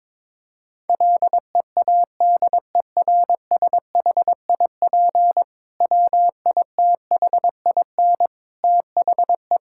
Morse code, heard as a keyed tone, about 22 wpm, 700 Hz. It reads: LEADERSHIP WITHIN THE
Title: Sentences from Top 1600 Words 22wpm